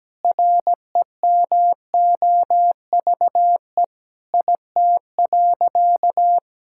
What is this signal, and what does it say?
Morse code, heard as a keyed tone, about 17 wpm, 700 Hz. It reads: REMOVE IT.
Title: Sentences from Top 1000 Words - Review Part 5 - 17wpm